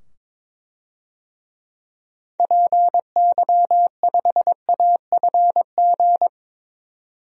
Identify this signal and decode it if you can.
Morse code, heard as a keyed tone, about 22 wpm, 700 Hz. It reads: PY5AFG
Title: DXCC-100 Call Signs - 6 Characters - 22wpm